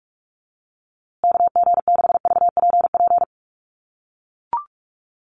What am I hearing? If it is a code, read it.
Morse code, heard as a keyed tone, about 44 wpm, 690 Hz. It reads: KG6VPP
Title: USA Call Signs Repeated 3x 45wpm